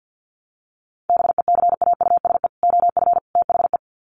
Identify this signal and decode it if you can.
Morse code, heard as a keyed tone, about 50 wpm, 700 Hz. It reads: BECAUSE OF THE
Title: Sets of 3 Words 50wpm